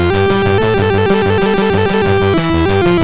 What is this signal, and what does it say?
There is no code.